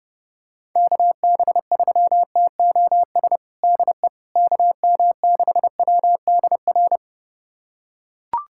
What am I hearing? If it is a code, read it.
Morse code, heard as a keyed tone, about 32 wpm, 700 Hz. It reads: KB3TOS DE KM6WDR
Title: USA Callsign DE Callsign - 6 Characters - 30wpm